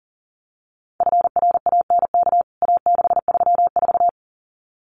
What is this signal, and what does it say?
Morse code, heard as a keyed tone, about 40 wpm, 700 Hz. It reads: FRANK A634
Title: CWT Exchange 40wpm